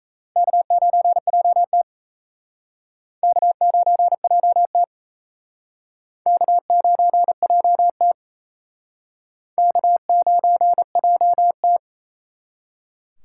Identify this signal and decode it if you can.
Morse code, heard as a keyed tone, about 34 wpm, 700 Hz. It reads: K9JT K9JT K9JT K9JT
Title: Speed Racing - USA Call Signs 28wpm